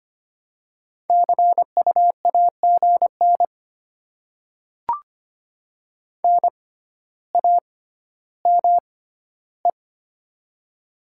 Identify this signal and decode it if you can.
Morse code, heard as a keyed tone, about 25 wpm, spaced as for 13 wpm, 700 Hz. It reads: CU AGN NAME
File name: Top 20 QSO Elements - 6x Character Spacing - 25wpm